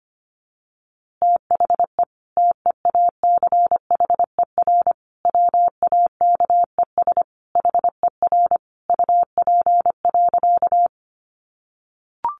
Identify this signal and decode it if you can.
Morse code, heard as a keyed tone, about 25 wpm, 700 Hz. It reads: THE TEACHER WAKES HER UP.
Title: Easy Phrases 25wpm